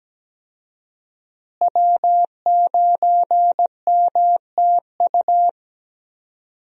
Morse code, heard as a keyed tone, about 17 wpm, 700 Hz. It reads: W9MTU
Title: USA Call Signs - 5 Characters - Repeated 3x - 17wpm